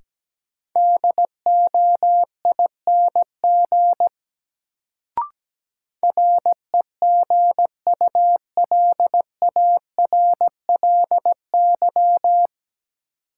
Morse code, heard as a keyed tone, about 17 wpm, 700 Hz. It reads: DOING REGULARLY
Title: Sentences from Top 1400 Words - New Word Drill Alt - 17wpm